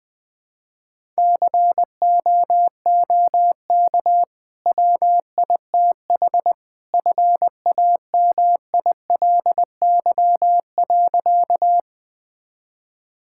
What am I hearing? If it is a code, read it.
Morse code, heard as a keyed tone, about 20 wpm, 700 Hz. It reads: COOK WITH FAMILY.
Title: Easy Sentences with Hints - 20wpm